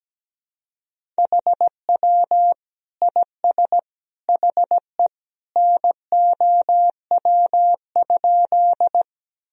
Morse code, heard as a keyed tone, about 17 wpm, 700 Hz. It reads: HW IS HE NOW?
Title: Sentences from Top 100 Words and Abbreviations 17wpm